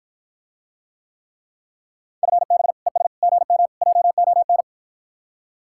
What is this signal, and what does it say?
Morse code, heard as a keyed tone, about 54 wpm, 700 Hz. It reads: FB ES GM JON